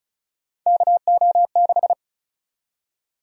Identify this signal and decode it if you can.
Morse code, heard as a keyed tone, about 36 wpm, 700 Hz. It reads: KO6
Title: USA Call Sign Prefixes - Repeated 2x - 35wpm